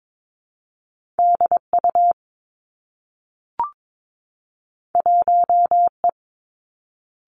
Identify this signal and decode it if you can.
Morse code, heard as a keyed tone, about 22 wpm, 700 Hz. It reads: D U 1 E
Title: Two Challenging Letter-Number Combinations 22wpm